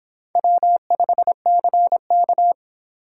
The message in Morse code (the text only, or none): W5CK